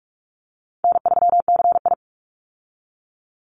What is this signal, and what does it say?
Morse code, heard as a keyed tone, about 44 wpm, 700 Hz. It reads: N3CI
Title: Easy Call Signs Repeated 2x - 45wpm